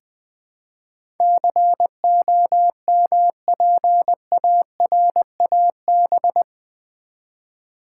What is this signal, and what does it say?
Morse code, heard as a keyed tone, about 20 wpm, 700 Hz. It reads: COMPARAB
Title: Build A Word - 4 Letters At A Time - 10-Letter Words - 20wpm